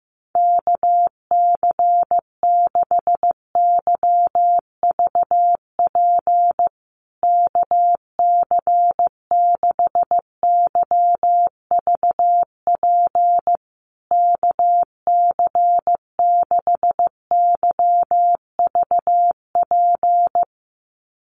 Morse code, heard as a keyed tone, about 15 wpm, 700 Hz. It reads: KC6YVP KC6YVP KC6YVP
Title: USA Call Signs Repeated 3x 15wpm